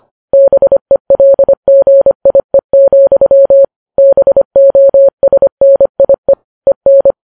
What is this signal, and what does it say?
Morse code, heard as a keyed tone, about 25 wpm, 550 Hz. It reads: BELGIE, BOSNIE EN
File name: Files Web tls / Morse on9fv / morse25kps